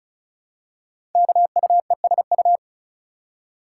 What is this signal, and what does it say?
Morse code, heard as a keyed tone, about 36 wpm, 700 Hz. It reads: KUESU
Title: Code Groups - 5 Letter:Numbers - 35wpm